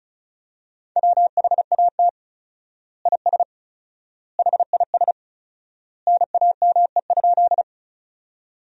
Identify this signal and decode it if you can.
Morse code, heard as a keyed tone, about 36 wpm, 700 Hz. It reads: WHAT IS HIS NAME?
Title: 4x Word-Spacing - Sentences from Top 200 Words - 35wpm